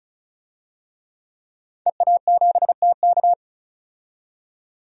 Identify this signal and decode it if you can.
Morse code, heard as a keyed tone, about 36 wpm, 700 Hz. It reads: EA7TK
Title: DXCC-100 Call Signs - 5 Characters - 35wpm